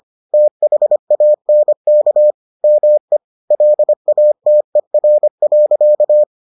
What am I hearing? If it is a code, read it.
Morse code, heard as a keyed tone, about 25 wpm, 600 Hz. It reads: THANK ME LATER.